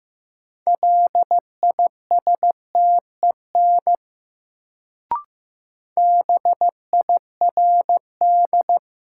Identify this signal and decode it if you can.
Morse code, heard as a keyed tone, about 15 wpm, 700 Hz. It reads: LISTEN BIRD
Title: Sentences from Top 400 Words - New Word Drill - 15wpm